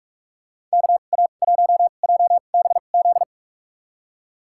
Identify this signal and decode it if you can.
Morse code, heard as a keyed tone, about 46 wpm, spaced as for 32 wpm, 700 Hz. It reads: KA1JBZ
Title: USA Call Signs - 2x Character Spacing - 45wpm